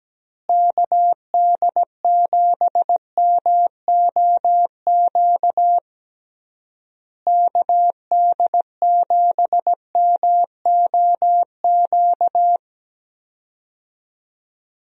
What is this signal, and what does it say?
Morse code, heard as a keyed tone, about 17 wpm, 700 Hz. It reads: KD7MOQ KD7MOQ
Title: USA Call Signs - 6 Characters - Repeated 3x - 17wpm